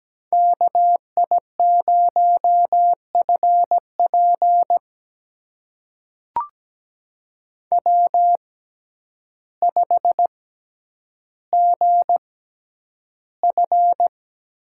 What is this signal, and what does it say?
Morse code, heard as a keyed tone, about 17 wpm, spaced as for 9 wpm, 700 Hz. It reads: KI0FP W5GF
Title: USA Call Signs - 5 Characters - 6x Character Spacing - 17wpm